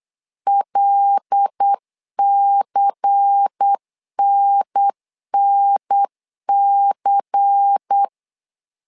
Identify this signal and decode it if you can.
Morse code, heard as a keyed tone, about 8 wpm, 800 Hz. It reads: LCNNC
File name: LEZIONI / 5 LBVC / 5 LBVC 8wpm 2935